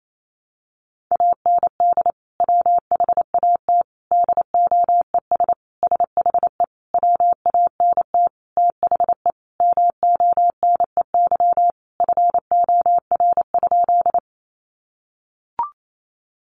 Morse code, heard as a keyed tone, about 28 wpm, 700 Hz. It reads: AND WHAT DOES SHE WANT THE MONEY FOR?
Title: Sentences from Top 400 Words 28wpm